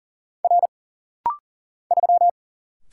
Morse code, heard as a keyed tone, about 40 wpm, 700 Hz. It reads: R 3
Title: Single Letter-Number - Rapid-Fire 40wpm